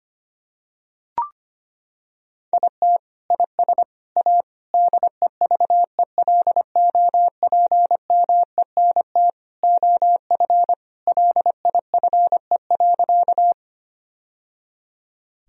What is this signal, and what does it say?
Morse code, heard as a keyed tone, about 25 wpm, 700 Hz. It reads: IT IS A DEVELOPMENT OF LIFE.
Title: Sentences from Top 1000 Words - Review Part 4 - 25wpm